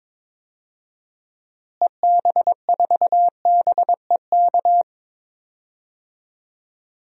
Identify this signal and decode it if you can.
Morse code, heard as a keyed tone, about 22 wpm, 700 Hz. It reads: EB4BEK
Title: DXCC-100 Call Signs - 6 Characters - Repeated 3x - 22wpm